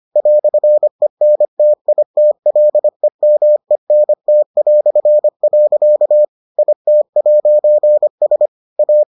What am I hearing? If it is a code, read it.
Morse code, heard as a keyed tone, about 25 wpm, 600 Hz. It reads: "ENTITLEMENT". IT'S A